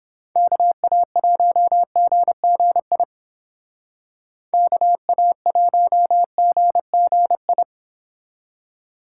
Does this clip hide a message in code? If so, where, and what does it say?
Yes: Morse code from 0.4 to 7.6 s, heard as a keyed tone, about 29 wpm, 700 Hz. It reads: KA1GGI KA1GGI